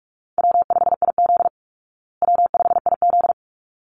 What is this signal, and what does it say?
Morse code, heard as a keyed tone, about 44 wpm, 700 Hz. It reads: W5IZ W5IZ
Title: Call Signs Repeated 3x - 45wpm